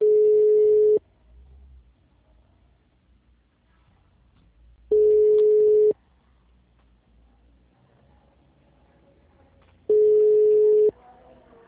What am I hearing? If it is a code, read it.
Ringback tone (Europe, 425 Hz), with speech.
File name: out- -2017- - - .50167